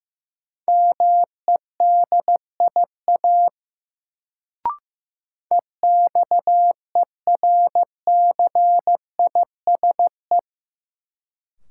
Morse code, heard as a keyed tone, about 15 wpm, 700 Hz. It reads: MEDIA EXERCISE
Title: Sentences from Top 1000 Words - Review Part 4 - New Word Drill - 15wpm